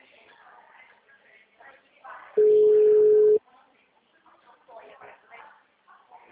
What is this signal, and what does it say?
Ringback tone (Europe, 425 Hz), with speech.